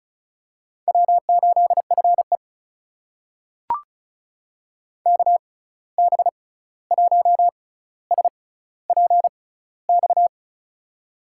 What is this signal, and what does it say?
Morse code, heard as a keyed tone, about 34 wpm, spaced as for 16 wpm, 700 Hz. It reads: W8FE KB1SPX
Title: USA Call Signs - 6x Character Spacing - 35wpm